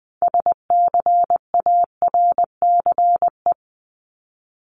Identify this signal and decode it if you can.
Morse code, heard as a keyed tone, about 20 wpm, 700 Hz. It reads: SCARCE
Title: LY Suffix - Top 100 Words with Base Words 20wpm